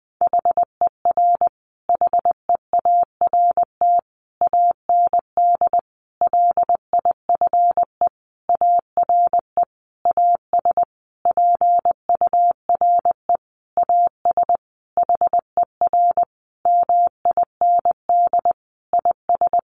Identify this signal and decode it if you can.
Morse code, heard as a keyed tone, about 20 wpm, 700 Hz. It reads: HER HEART AND LIFE ARE AS PURE AS HER MIND IS
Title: Sentences from Top 1500 Words 20wpm